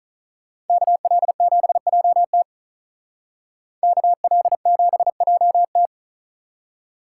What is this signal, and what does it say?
Morse code, heard as a keyed tone, about 43 wpm, 700 Hz. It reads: KL7JT KL7JT
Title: Speed Racing - USA 2x2 Call Signs 35wpm